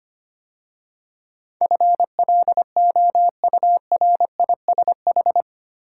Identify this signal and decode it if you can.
Morse code, heard as a keyed tone, about 25 wpm, 700 Hz. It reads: FLOURISH
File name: 8 Letter Words - First 3 Letters Given - 25wpm